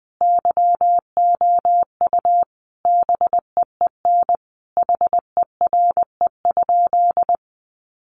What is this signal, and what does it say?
Morse code, heard as a keyed tone, about 20 wpm, 700 Hz. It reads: YOU BEEN HERE?
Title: Sentences from Top 200 Words 20wpm